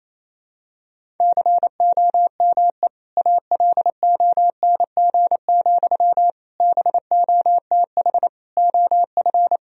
Morse code, heard as a keyed tone, about 28 wpm, 700 Hz. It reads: COME ALONG, BOTH OF
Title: Sentences from Top 1000 Words - Review Part 1 - 28wpm